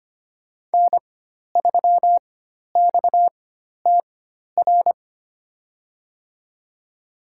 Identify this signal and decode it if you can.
Morse code, heard as a keyed tone, about 25 wpm, spaced as for 12 wpm, 700 Hz. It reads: N3XTR
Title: USA Call Signs - 4x Character Spacing - 25wpm